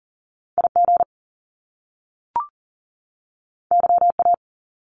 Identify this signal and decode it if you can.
Morse code, heard as a keyed tone, about 41 wpm, 700 Hz. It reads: IG YA
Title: Top Two Letter Combinations in Words 40wpm